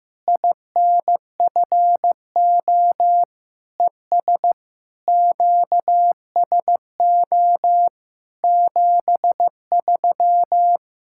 Morse code, heard as a keyed tone, about 15 wpm, 700 Hz. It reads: INFO ES QSO 73
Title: Practice QSO - Exchange 4 - Ending - 15wpm